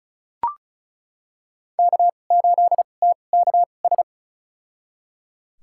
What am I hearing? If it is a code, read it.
Morse code, heard as a keyed tone, about 34 wpm, spaced as for 25 wpm, 700 Hz. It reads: K8TKS
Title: USA Call Signs - 5 Characters - 2x Character Spacing - 35wpm